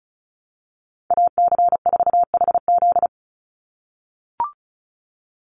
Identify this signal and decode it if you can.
Morse code, heard as a keyed tone, about 34 wpm, 700 Hz. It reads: AC4HZ
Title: USA Call Signs 35wpm